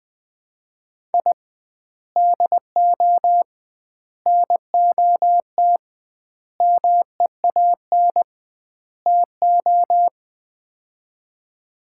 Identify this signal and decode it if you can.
Morse code, heard as a keyed tone, about 20 wpm, 700 Hz. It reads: I DO NOT MEAN TO
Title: 2x Word-Spacing - Sets of 5 Words - 20wpm